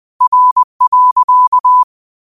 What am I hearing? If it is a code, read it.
Morse code, heard as a keyed tone, about 20 wpm, 1000 Hz. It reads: R.
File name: qotdfiles / 20wpm